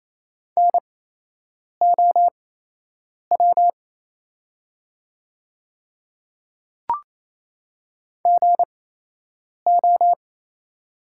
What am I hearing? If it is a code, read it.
Morse code, heard as a keyed tone, about 28 wpm, spaced as for 8 wpm, 700 Hz. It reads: NOW GO
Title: 8x Character Spacing - Top 100 Words - 28wpm